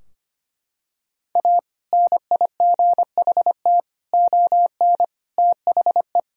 Morse code, heard as a keyed tone, about 25 wpm, 700 Hz. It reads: A NIGHT ON THE
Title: Sentences from Top 1000 Words - Review Part 3 - 25wpm